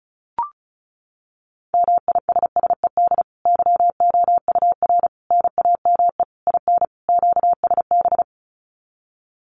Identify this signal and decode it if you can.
Morse code, heard as a keyed tone, about 34 wpm, 700 Hz. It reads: MISSED YOUR NAME IN QSB
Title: Common QSO Phrases 35wpm